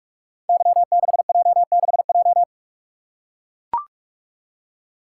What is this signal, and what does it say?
Morse code, heard as a keyed tone, about 44 wpm, 700 Hz. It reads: Y6J6J
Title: Code Groups - 5 Letter:Numbers - 45wpm